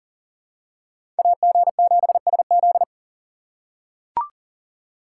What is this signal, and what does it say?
Morse code, heard as a keyed tone, about 38 wpm, 700 Hz. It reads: AG7SZ
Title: USA Call Signs - 5 Characters - 4x Character Spacing - 40wpm